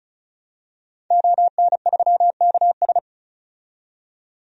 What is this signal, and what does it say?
Morse code, heard as a keyed tone, about 34 wpm, 700 Hz. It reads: ON3KS
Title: DXCC-100 Call Signs - 5 Characters - 35wpm